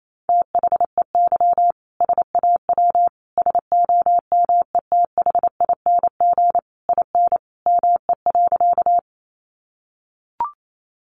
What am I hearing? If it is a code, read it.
Morse code, heard as a keyed tone, about 28 wpm, 700 Hz. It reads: THEY SAW SOMETHING IN ME.
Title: Sentences from Top 600 Words 28wpm